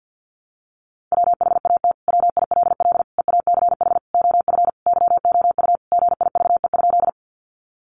Morse code, heard as a keyed tone, about 50 wpm, 700 Hz. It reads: WHAT WILL EACH OF YOU GIVE?
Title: Sentences from Top 200 Words 50wpm